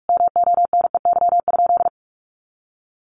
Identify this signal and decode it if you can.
Morse code, heard as a keyed tone, about 44 wpm, 700 Hz. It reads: MONEY?
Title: Sentences from Top 1400 Words 45wpm